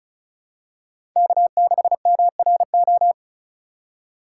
Morse code, heard as a keyed tone, about 36 wpm, 700 Hz. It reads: K6MRO
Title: USA Call Signs - 5 Characters - 6x Character Spacing - 35wpm